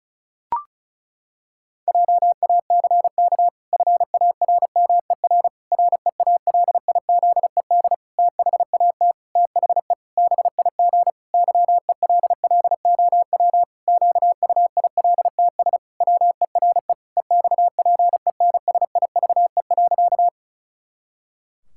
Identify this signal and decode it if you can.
Morse code, heard as a keyed tone, about 36 wpm, 700 Hz. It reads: JACK FARMER REALIZED THAT THE BIG YELLOW QUILTS WERE EXPENSIVE.
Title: Pangram Challenge v2 - 35wpm